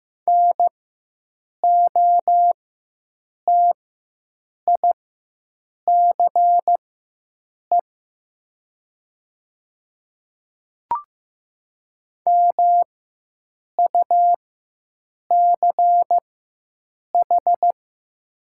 Morse code, heard as a keyed tone, about 15 wpm, spaced as for 7 wpm, 690 Hz. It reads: NOTICE MUCH
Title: 4x Character Spacing - Top 500 Words - 15wpm